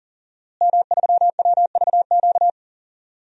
Morse code, heard as a keyed tone, about 40 wpm, 700 Hz. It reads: M3WVQ
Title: DXCC-100 Call Signs - 5 Characters - Repeated 2x - 40wpm